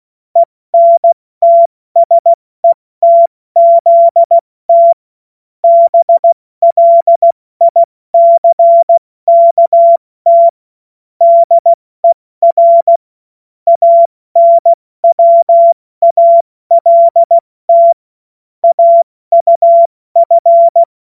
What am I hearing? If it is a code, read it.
Morse code, heard as a keyed tone, about 16 wpm, 680 Hz. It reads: ENTSETZT BLICKT DER ANWALT AUF
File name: Morsekurs / Witze / Witz 17. Mai 2022 70 F80